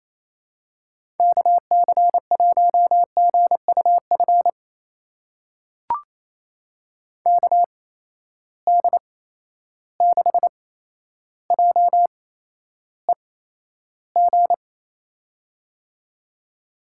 Morse code, heard as a keyed tone, about 28 wpm, spaced as for 12 wpm, 700 Hz. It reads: KC1GUF KD6JEG
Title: USA Call Signs - 8x Character Spacing - 28wpm